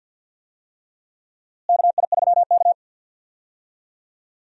Morse code, heard as a keyed tone, about 53 wpm, 700 Hz. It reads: KI3K
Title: USA Call Signs - 4 Characters - Repeated 3x - 50wpm